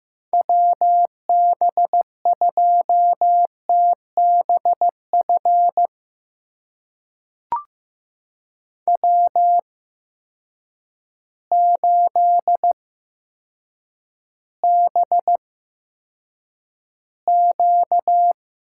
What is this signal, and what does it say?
Morse code, heard as a keyed tone, about 16 wpm, spaced as for 8 wpm, 700 Hz. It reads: WB2TBF W8BQ
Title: USA Call Signs - 8x Character Spacing - 15wpm